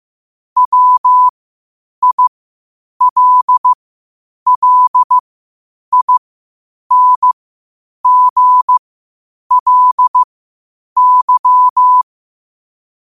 Morse code, heard as a keyed tone, about 15 wpm, spaced as for 8 wpm, 1000 Hz. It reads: WILLINGLY